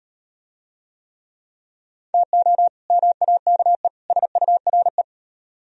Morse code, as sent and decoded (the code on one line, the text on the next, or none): - --- / -- .- -.- . / ... ..- .-. .
TO MAKE SURE